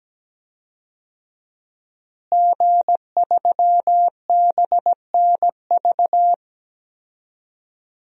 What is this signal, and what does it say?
Morse code, heard as a keyed tone, about 17 wpm, 700 Hz. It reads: G3BNV
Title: DXCC-100 Call Signs - 5 Characters - Repeated 3x - 17wpm